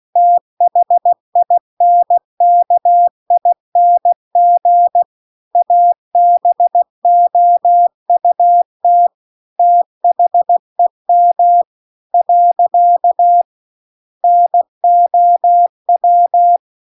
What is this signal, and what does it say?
Morse code, heard as a keyed tone, about 16 wpm, 700 Hz. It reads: THINKING ABOUT THEM . NOW